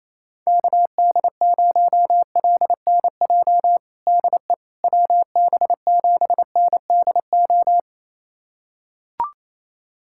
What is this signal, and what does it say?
Morse code, heard as a keyed tone, about 28 wpm, 700 Hz. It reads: KD0LNJ DE WB7NDO
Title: USA Callsign DE Callsign 28wpm